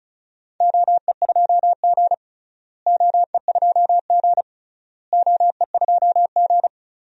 Morse code, heard as a keyed tone, about 35 wpm, 700 Hz. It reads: OE2G OE2G OE2G
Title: DXCC-100 Call Signs - 4 Characters - Repeated 3x - 35wpm